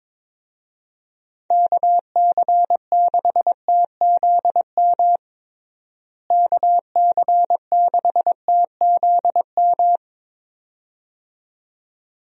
Morse code, heard as a keyed tone, about 22 wpm, 700 Hz. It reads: KC6TZM KC6TZM
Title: USA Call Signs - 6 Characters - Repeated 2x - 22wpm